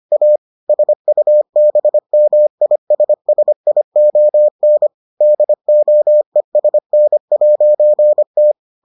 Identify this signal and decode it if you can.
Morse code, heard as a keyed tone, about 25 wpm, 600 Hz. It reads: A SUBMISSION DOESN'T